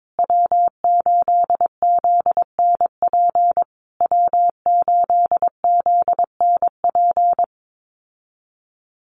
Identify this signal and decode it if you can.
Morse code, heard as a keyed tone, about 22 wpm, 700 Hz. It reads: W8ZNP W8ZNP
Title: USA Call Signs Repeated 3x 22wpm